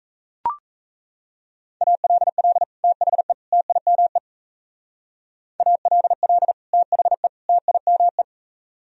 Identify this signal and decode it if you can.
Morse code, heard as a keyed tone, about 35 wpm, 700 Hz. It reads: ALL THE TIME ALL THE TIME